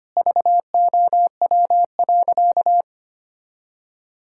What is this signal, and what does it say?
Morse code, heard as a keyed tone, about 25 wpm, 700 Hz. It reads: VOW.
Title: Pangram Challenge v2 - 25wpm